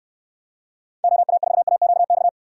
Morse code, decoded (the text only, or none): OA4AQX